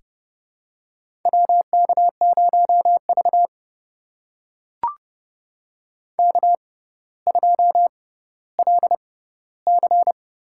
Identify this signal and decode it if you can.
Morse code, heard as a keyed tone, about 30 wpm, spaced as for 15 wpm, 700 Hz. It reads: WK0V K2LC
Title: USA Call Signs - 4 Characters - 6x Character Spacing - 30wpm